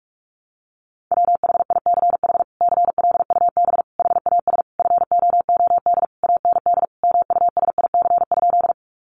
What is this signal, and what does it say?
Morse code, heard as a keyed tone, about 44 wpm, 700 Hz. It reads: WHICH CLUB HAS FOOD AND MUSIC?
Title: Sentences from Top 1200 Words 45wpm